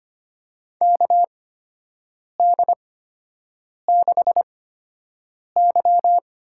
Morse code, heard as a keyed tone, about 25 wpm, spaced as for 7 wpm, 700 Hz. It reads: KD6Y
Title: USA Call Signs - 6 Characters - 8x Character Spacing - 25wpm